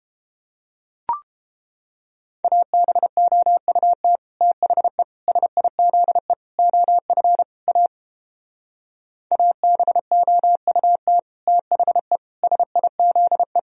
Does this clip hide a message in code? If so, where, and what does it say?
Yes: Morse code from 2.4 to 13.6 s, heard as a keyed tone, about 29 wpm, 700 Hz. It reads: ABOUT THE SIZE OF A ABOUT THE SIZE